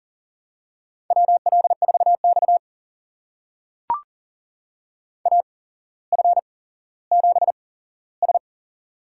Morse code, heard as a keyed tone, about 39 wpm, spaced as for 17 wpm, 700 Hz. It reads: WL4X AF7S
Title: USA Call Signs - 8x Character Spacing - 40wpm